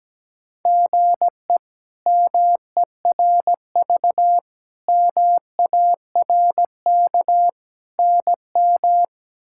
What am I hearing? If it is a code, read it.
Morse code, heard as a keyed tone, about 17 wpm, 700 Hz. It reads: GE MERV MARK NM
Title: K1USN SST QSO Practice - 17wpm